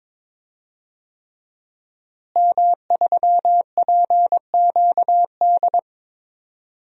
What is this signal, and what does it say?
Morse code, heard as a keyed tone, about 22 wpm, 700 Hz. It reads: M3PQD
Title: DXCC-100 Call Signs - 5 Characters - Repeated 3x - 22wpm